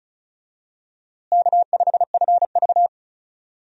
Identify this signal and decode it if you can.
Morse code, heard as a keyed tone, about 36 wpm, 700 Hz. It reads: K5FV
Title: USA Call Signs - 4 Characters - Repeated 3x - 35wpm